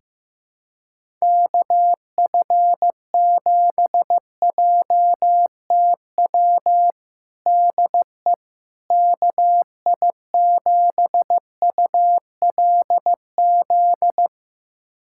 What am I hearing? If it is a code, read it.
Morse code, heard as a keyed tone, about 15 wpm, 700 Hz. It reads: KF7JTW DE KI7ULZ
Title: USA Callsign DE Callsign 15wpm